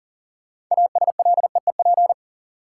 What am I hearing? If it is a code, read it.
Morse code, heard as a keyed tone, about 40 wpm, 700 Hz. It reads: ASLEEP
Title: Six Letter Words - 40wpm